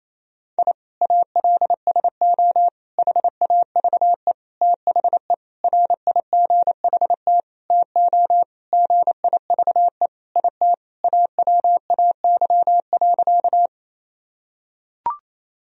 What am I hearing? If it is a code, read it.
Morse code, heard as a keyed tone, about 28 wpm, 700 Hz. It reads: I ALSO HAVE THE RIGHT TO GIVE IT AWAY.
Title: Sentences from Top 1000 Words - Review Part 1 - 28wpm